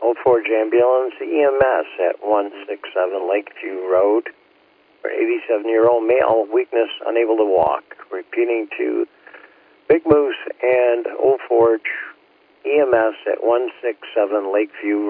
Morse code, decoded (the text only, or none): none